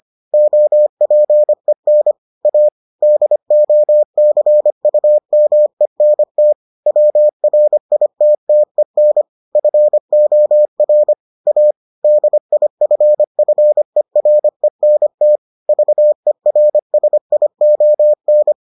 Morse code, heard as a keyed tone, about 25 wpm, 600 Hz. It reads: OPEN A DOCUMENT WRITTEN FOR A DIFFERENT VERSION